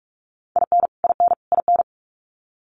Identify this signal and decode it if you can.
Morse code, heard as a keyed tone, about 44 wpm, 700 Hz. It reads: IN IN IN
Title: Top 200 Words Repeated 45wpm